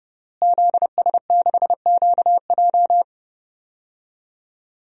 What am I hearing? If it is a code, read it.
Morse code, heard as a keyed tone, about 30 wpm, 700 Hz. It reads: ZS6QJ
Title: DXCC-100 Call Signs - 5 Characters - Repeated 2x - 30wpm